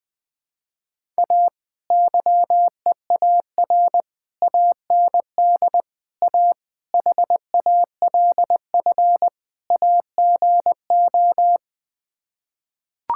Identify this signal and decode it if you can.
Morse code, heard as a keyed tone, about 20 wpm, 700 Hz. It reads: A YEAR AND A HALF AGO
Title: Sets of 6 Words - 20wpm